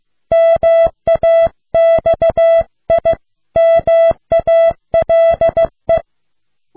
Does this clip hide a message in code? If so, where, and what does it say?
Yes: Morse code from 0.3 to 6.0 s, heard as a keyed tone, about 15 wpm, 660 Hz. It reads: MAXIMALE